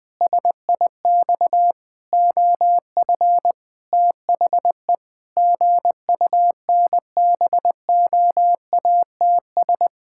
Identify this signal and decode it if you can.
Morse code, heard as a keyed tone, about 21 wpm, 700 Hz. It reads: SIX OF THE GUNBOATS
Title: Pangram Challenge v2 - 20wpm